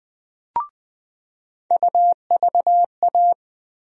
Morse code, heard as a keyed tone, about 20 wpm, 700 Hz. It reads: UVA
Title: USA Call Sign Postfixes - 3 Letters 20wpm